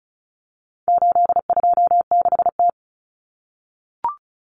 Morse code, heard as a keyed tone, about 34 wpm, 700 Hz. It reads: 826T
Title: Common Cut Numbers - 4-Digit 35wpm